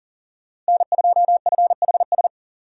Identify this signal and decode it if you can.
Morse code, heard as a keyed tone, about 40 wpm, 700 Hz. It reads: N2FHS
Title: USA Call Signs - 5 Characters - 6x Character Spacing - 40wpm